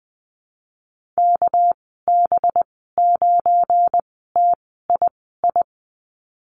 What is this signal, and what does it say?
Morse code, heard as a keyed tone, about 20 wpm, spaced as for 15 wpm, 700 Hz. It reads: KB9TII
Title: 238 Lesson 38g - Letter q - Call Signs 20wpm